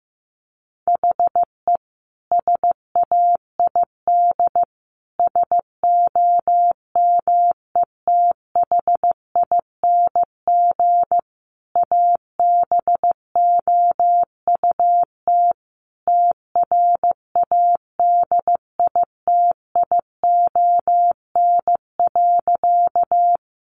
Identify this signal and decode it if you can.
Morse code, heard as a keyed tone, about 15 wpm, 700 Hz. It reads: HE SAID SOMETHING ABOUT TRADITION.
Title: Sentences from Top 1500 Words 15wpm